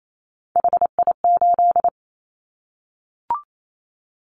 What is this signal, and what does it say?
Morse code, heard as a keyed tone, about 28 wpm, 700 Hz. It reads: HI8
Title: Callsign Prefixes for Top 100 DXCC - 28wpm